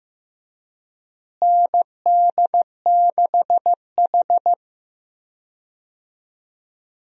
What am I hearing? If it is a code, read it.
Morse code, heard as a keyed tone, about 15 wpm, 700 Hz. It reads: ND6H
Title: USA Call Signs - 4 Characters - Repeated 2x - 15wpm